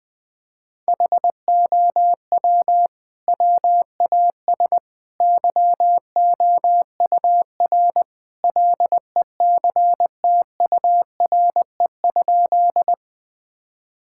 Morse code, heard as a keyed tone, about 21 wpm, 700 Hz. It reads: HOW WAS YOUR LECTURE?
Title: Sentences from Top 1600 Words 20wpm